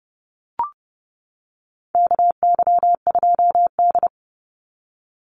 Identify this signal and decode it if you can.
Morse code, heard as a keyed tone, about 30 wpm, 700 Hz. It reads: KY2D
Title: Call Signs - 30wpm